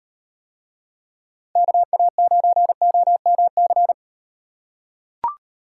Morse code, heard as a keyed tone, about 38 wpm, 700 Hz. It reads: KA9OMC